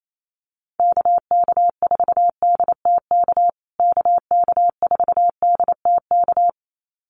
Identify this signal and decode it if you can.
Morse code, heard as a keyed tone, about 28 wpm, 700 Hz. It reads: KK4DTK KK4DTK
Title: USA Call Signs Repeated 2x 28wpm